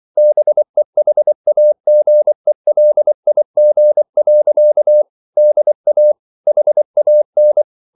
Morse code, heard as a keyed tone, about 24 wpm, 600 Hz. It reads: BEHAGELIG. DA HAN
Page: Ferie 24 wpm